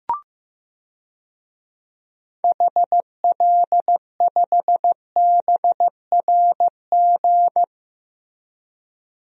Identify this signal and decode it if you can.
Morse code, heard as a keyed tone, about 15 wpm, 700 Hz. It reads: HL5BRG
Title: DXCC-100 Call Signs - 6 Characters - Repeated 3x - 15wpm